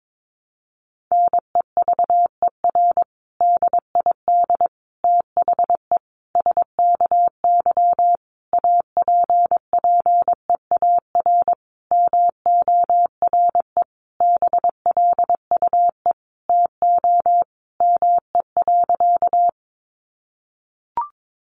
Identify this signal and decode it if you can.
Morse code, heard as a keyed tone, about 22 wpm, 700 Hz. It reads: NEVER DID THE SKY APPEAR MORE BLUE TO ME.
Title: Sentences from Top 1200 Words 22wpm